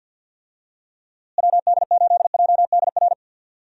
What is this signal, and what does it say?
Morse code, heard as a keyed tone, about 52 wpm, 700 Hz. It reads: WD8JDR
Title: USA Call Signs - 6 Characters - Repeated 3x - 50wpm